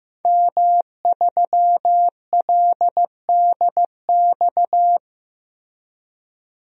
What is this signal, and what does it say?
Morse code, heard as a keyed tone, about 15 wpm, 700 Hz. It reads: M3LDX
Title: DXCC-100 Call Signs - 5 Characters - Repeated 2x - 15wpm